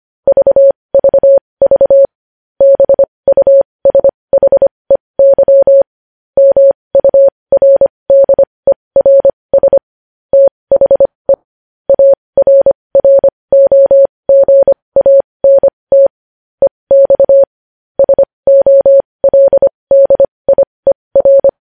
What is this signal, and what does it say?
Morse code, heard as a keyed tone, about 25 wpm, spaced as for 20 wpm, 550 Hz. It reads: VVV BUSHEY MURDERS THE ARROGANT EX SOLDIER
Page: News Headlines in Morse Code at 20 WPM